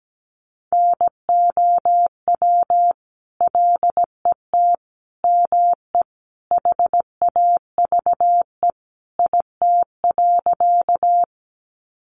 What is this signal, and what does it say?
Morse code, heard as a keyed tone, about 17 wpm, 700 Hz. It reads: NOW LET ME HAVE IT.
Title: Sentences from Top 300 Words 17wpm